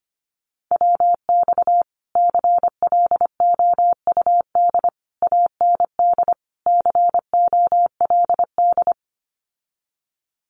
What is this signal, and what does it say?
Morse code, heard as a keyed tone, about 25 wpm, 700 Hz. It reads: WX CLOUD AND COLD
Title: Common QSO Phrases Encore 25wpm